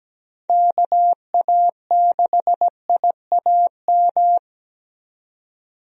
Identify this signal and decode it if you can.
Morse code, heard as a keyed tone, about 17 wpm, 700 Hz. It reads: KA6IAM
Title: USA Call Signs - 6 Characters - 17wpm